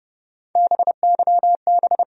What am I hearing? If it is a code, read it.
Morse code, heard as a keyed tone, about 30 wpm, 700 Hz. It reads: BYB
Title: USA Call Sign Postfixes - 3 Letters 30wpm